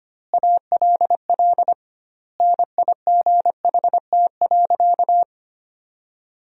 Morse code, heard as a keyed tone, about 25 wpm, 700 Hz. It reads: ALL NIGHT.
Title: 2x Word-Spacing - Sentences from Top 300 Words - 25wpm